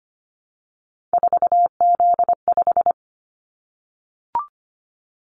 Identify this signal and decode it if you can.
Morse code, heard as a keyed tone, about 25 wpm, 700 Hz. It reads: 4Z5
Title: Callsign Prefixes for Top 50 DXCC - 25wpm